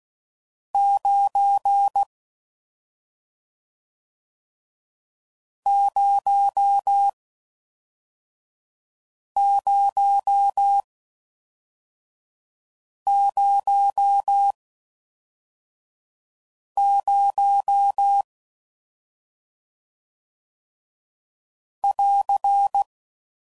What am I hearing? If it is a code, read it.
Morse code, heard as a keyed tone, about 16 wpm, 800 Hz. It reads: 9 0 0 0 0 <AR>